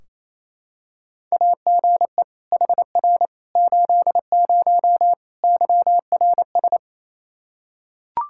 Morse code, heard as a keyed tone, about 28 wpm, 700 Hz. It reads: AGE HR 80 YRS
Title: K1USN SST QSO Practice 28wpm